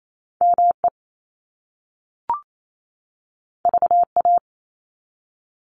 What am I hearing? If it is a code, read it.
Morse code, heard as a keyed tone, about 28 wpm, 700 Hz. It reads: ME VA
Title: ARRL Sections 28wpm